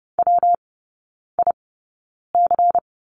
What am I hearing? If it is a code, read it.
Morse code, heard as a keyed tone, about 30 wpm, spaced as for 9 wpm, 700 Hz. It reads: WIC
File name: Lesson 19e - Letter w - Three Characters 30wpm